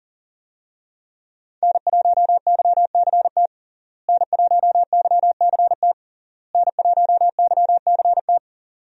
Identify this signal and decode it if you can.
Morse code, heard as a keyed tone, about 40 wpm, 700 Hz. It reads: N1YCT N1YCT N1YCT
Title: USA Call Signs - 5 Characters - Repeated 3x - 40wpm